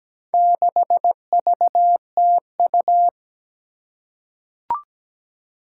Morse code, heard as a keyed tone, about 17 wpm, 700 Hz. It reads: 6VTU
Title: Four Challenging Letter-Number Combinations 17wpm